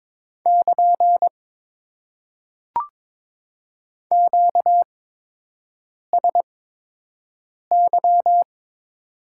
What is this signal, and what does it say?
Morse code, heard as a keyed tone, about 22 wpm, spaced as for 6 wpm, 700 Hz. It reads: <KN> QSY
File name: Top 90 QSO Elements - 8x Character Spacing - 22wpm